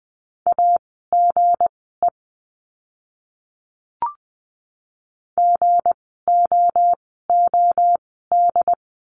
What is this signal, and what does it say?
Morse code, heard as a keyed tone, about 20 wpm, spaced as for 14 wpm, 700 Hz. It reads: AGE GOOD
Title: 160 Lesson 26f - Letter g - Words 20wpm